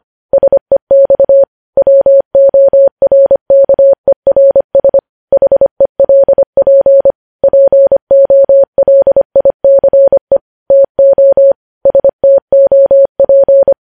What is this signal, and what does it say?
Morse code, heard as a keyed tone, about 25 wpm, 550 Hz. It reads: SEX WORKERS HELP POLICE TO STOP
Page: news headlines in morse code at 25 wpm